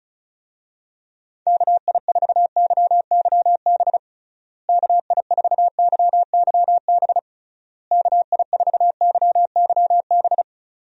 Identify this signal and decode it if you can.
Morse code, heard as a keyed tone, about 36 wpm, 700 Hz. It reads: KI4YYB KI4YYB KI4YYB
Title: USA Call Signs - 6 Characters - Repeated 3x - 35wpm